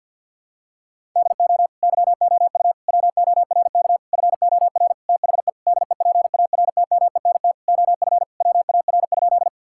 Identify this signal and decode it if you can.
Morse code, heard as a keyed tone, about 47 wpm, 700 Hz. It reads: DO YOU WORK FOR THE DEPARTMENT OF WAR?
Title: Sentences from Top 1000 Words - Review Part 5 - 50wpm